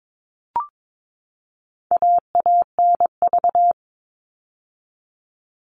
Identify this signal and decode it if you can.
Morse code, heard as a keyed tone, about 22 wpm, 700 Hz. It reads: AANV
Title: All Cut Numbers - 4-Digit 22wpm